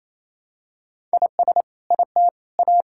Morse code, heard as a keyed tone, about 28 wpm, 700 Hz. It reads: IS IT A
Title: Sentences from Top 1000 Words - Review Part 6 - 28wpm